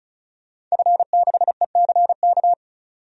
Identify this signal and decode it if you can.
Morse code, heard as a keyed tone, about 36 wpm, 700 Hz. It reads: F6ECK
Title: DXCC-100 Call Signs - 5 Characters - 35wpm